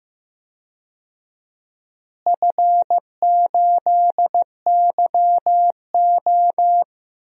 Morse code, heard as a keyed tone, about 15 wpm, 700 Hz. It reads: F8YO
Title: DXCC-100 Call Signs - 4 Characters - Repeated 2x - 15wpm